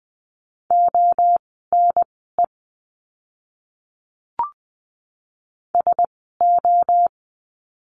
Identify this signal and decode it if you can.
Morse code, heard as a keyed tone, about 20 wpm, spaced as for 14 wpm, 700 Hz. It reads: ONE SO
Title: 230 Lesson 37f - Letter x - Words 20wpm